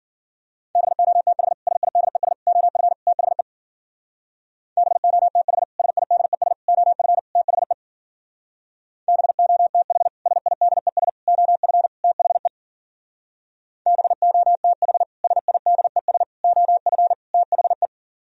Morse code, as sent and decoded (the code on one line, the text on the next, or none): -... --- - .... / ... .. -.. ... / --- ..-. / - .... / -... --- - .... / ... .. -.. ... / --- ..-. / - .... / -... --- - .... / ... .. -.. ... / --- ..-. / - .... / -... --- - .... / ... .. -.. ... / --- ..-. / - .... .
BOTH SIDES OF THE BOTH SIDES OF THE BOTH SIDES OF THE BOTH SIDES OF THE